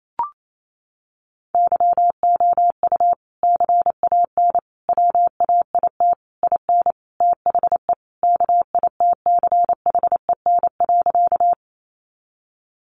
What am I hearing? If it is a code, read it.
Morse code, heard as a keyed tone, about 28 wpm, 700 Hz. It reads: YOU CAN WAIT IN THE KITCHEN.
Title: Sentences from Top 1400 Words 28wpm